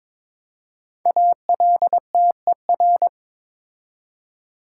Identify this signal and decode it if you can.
Morse code, heard as a keyed tone, about 22 wpm, 690 Hz. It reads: ALTER
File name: 5 Letter Words - First 3 Letters Given - 22wpm